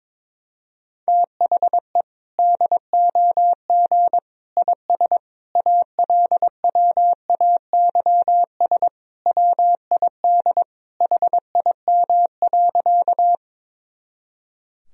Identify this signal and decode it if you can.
Morse code, heard as a keyed tone, about 22 wpm, 700 Hz. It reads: THE DOG IS ALWAYS WID HIM.
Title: Sentences from Top 400 Words and Abbreviations 22wpm